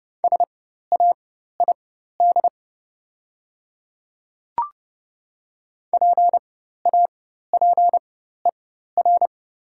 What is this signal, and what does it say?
Morse code, heard as a keyed tone, about 30 wpm, spaced as for 14 wpm, 700 Hz. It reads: SAID PAPER
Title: 4x Character Spacing - Top 300 Words - 30wpm